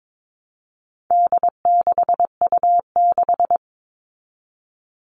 Morse code, heard as a keyed tone, about 22 wpm, 700 Hz. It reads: D6U6
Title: All Cut Numbers - 4-Digit 22wpm